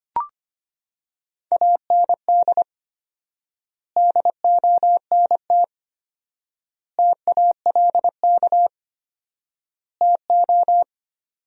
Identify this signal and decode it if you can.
Morse code, heard as a keyed tone, about 25 wpm, 700 Hz. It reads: AND DONT TALK TO
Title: 4x Word-Spacing - Sentences from Top 400 Words - 25wpm